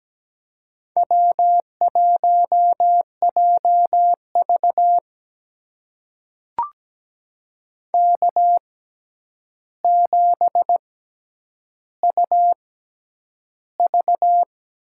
Morse code, heard as a keyed tone, about 17 wpm, spaced as for 9 wpm, 700 Hz. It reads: W1JV K7UV
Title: USA Call Signs - 4 Characters - 6x Character Spacing - 17wpm